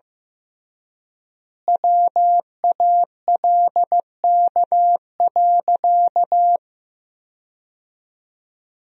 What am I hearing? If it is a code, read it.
Morse code, heard as a keyed tone, about 15 wpm, 700 Hz. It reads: WALK.
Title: 3x Word-Spacing - Sentences from Top 300 Words - 15wpm